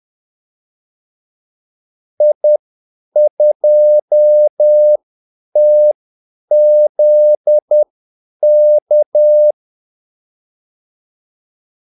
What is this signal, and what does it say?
Morse code, heard as a keyed tone, about 10 wpm, spaced as for 8 wpm, 600 Hz. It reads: I2TZK
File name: Callsigns-10-8wpm